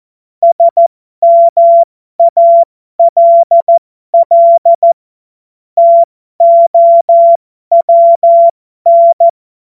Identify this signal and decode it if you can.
Morse code, heard as a keyed tone, about 14 wpm, spaced as for 12 wpm, 680 Hz. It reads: SMALL TOWN
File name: Morsekurs / Witze